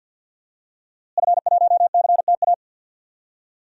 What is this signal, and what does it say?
Morse code, heard as a keyed tone, about 52 wpm, 700 Hz. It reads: F1CTA
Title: DXCC-100 Call Signs - 5 Characters - Repeated 2x - 50wpm